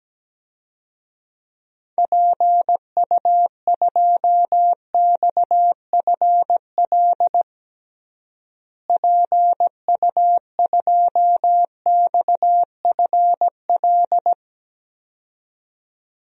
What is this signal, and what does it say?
Morse code, heard as a keyed tone, about 17 wpm, 700 Hz. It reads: PU2XFL PU2XFL
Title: DXCC-100 Call Signs - 6 Characters - Repeated 2x - 17wpm